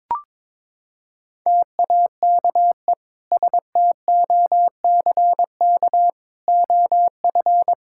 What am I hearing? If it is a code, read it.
Morse code, heard as a keyed tone, about 21 wpm, 700 Hz. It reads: TAKE STOCK OF
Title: Sentences from Top 1000 Words - Review Part 5 - 22wpm